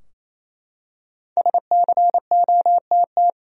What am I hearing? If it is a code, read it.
Morse code, heard as a keyed tone, about 27 wpm, 700 Hz. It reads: SCOTT
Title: USA Common Names - Max 5 Letters - 28wpm